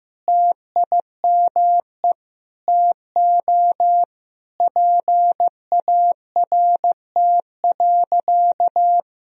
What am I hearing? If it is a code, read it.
Morse code, heard as a keyed tone, about 15 wpm, 700 Hz. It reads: TIME TO PART.
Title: Sentences from Top 1000 Words - Review Part 1 - 15wpm